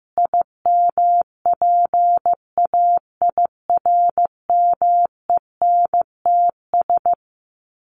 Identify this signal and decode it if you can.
Morse code, heard as a keyed tone, about 15 wpm, 700 Hz. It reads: IMPAIRMENTS
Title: Sets of 5 Words Encore 15wpm